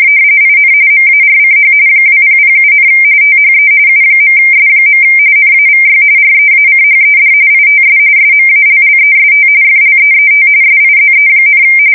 RTTY, about 45.45 baud. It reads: IN MARCH 2007, THE U.S. NAVY REACHED ITS SMALLEST FLEET SIZE,
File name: fsk170 / usnavy5